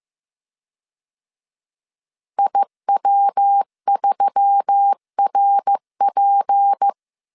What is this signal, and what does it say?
Morse code, heard as a keyed tone, about 15 wpm, 800 Hz. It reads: IW3RP
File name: CW / DET / 1 6 1500C S C 14wpm 2951